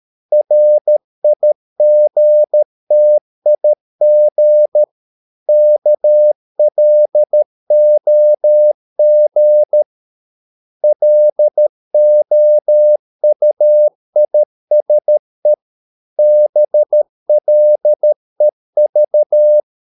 Morse code, heard as a keyed tone, about 13 wpm, 600 Hz. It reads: RIGTIG KLOG LOUISE BLEV
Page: Ferie 13 wpm